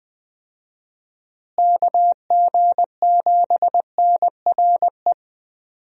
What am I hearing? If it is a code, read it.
Morse code, heard as a keyed tone, about 21 wpm, 700 Hz. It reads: KG7NRE
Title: USA Call Signs - 6 Characters - Repeated 3x - 20wpm